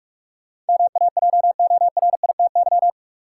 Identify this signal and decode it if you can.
Morse code, heard as a keyed tone, about 46 wpm, 700 Hz. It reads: MAJORITY
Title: Sets of 3 Words - New Word Drill - 45wpm